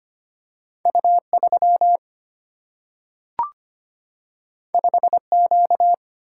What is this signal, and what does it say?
Morse code, heard as a keyed tone, about 25 wpm, 700 Hz. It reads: U3 5Q
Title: Single-Letter Single-Number 25wpm